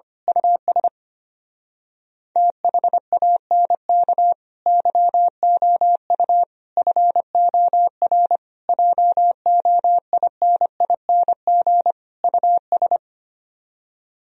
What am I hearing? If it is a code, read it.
Morse code, heard as a keyed tone, about 24 wpm, 700 Hz. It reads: US THANK YOU FOR JOINING US
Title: Speed-Racing - Sets of 5 Words 25wpm